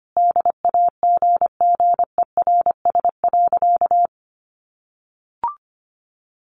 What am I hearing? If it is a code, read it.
Morse code, heard as a keyed tone, about 25 wpm, 700 Hz. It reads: DAGGERS.
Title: A Sound Of Thunder 25wpm